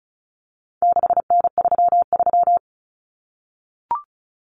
Morse code, heard as a keyed tone, about 34 wpm, 700 Hz. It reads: 6N33
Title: Common Cut Numbers - 4-Digit 35wpm